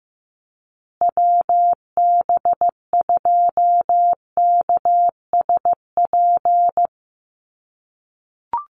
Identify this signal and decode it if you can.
Morse code, heard as a keyed tone, about 15 wpm, 700 Hz. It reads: WB2KSP
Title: USA Call Signs Repeated 3x 15wpm